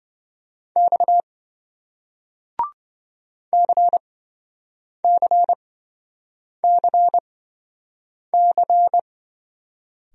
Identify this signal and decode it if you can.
Morse code, heard as a keyed tone, about 28 wpm, spaced as for 7 wpm, 700 Hz. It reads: X CCCC